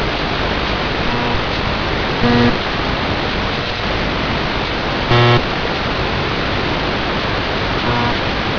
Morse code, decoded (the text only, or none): none